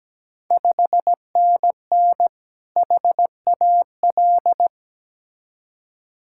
Morse code, heard as a keyed tone, about 17 wpm, 690 Hz. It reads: 5NN HAL
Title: Ontario QSO Party Exchange 17wpm